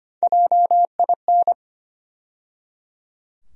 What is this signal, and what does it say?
Morse code, heard as a keyed tone, about 25 wpm, 700 Hz. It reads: JIN
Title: USA Call Sign Postfixes - 3 Letters 25wpm